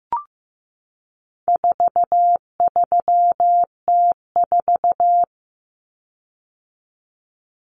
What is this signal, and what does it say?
Morse code, heard as a keyed tone, about 15 wpm, 700 Hz. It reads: 43T4
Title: Common Cut Numbers - 4-Digit 15wpm